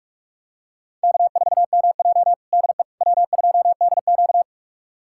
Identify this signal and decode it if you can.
Morse code, heard as a keyed tone, about 47 wpm, 700 Hz. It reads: K4MJ DE W2DQ
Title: USA Callsign DE Callsign - 4 Characters - 45wpm